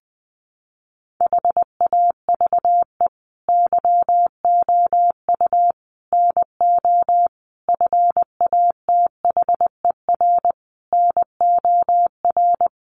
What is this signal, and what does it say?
Morse code, heard as a keyed tone, about 20 wpm, 700 Hz. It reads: HAVE YOU NO FATHER NOR
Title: Sentences from Top 1500 Words 20wpm